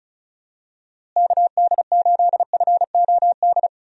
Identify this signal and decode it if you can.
Morse code, heard as a keyed tone, about 36 wpm, 700 Hz. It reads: KD8FOD
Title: USA Call Signs - 6 Characters - Repeated 2x - 35wpm